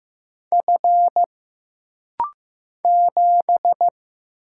Morse code, heard as a keyed tone, about 15 wpm, 700 Hz. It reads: F 7
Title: Single Letter-Number Alt - Rapid-Fire 15wpm